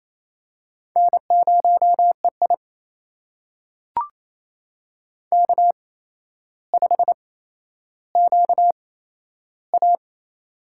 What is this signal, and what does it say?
Morse code, heard as a keyed tone, about 28 wpm, spaced as for 12 wpm, 700 Hz. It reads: N0EI K5QA
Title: USA Call Signs - 4 Characters - 8x Character Spacing - 28wpm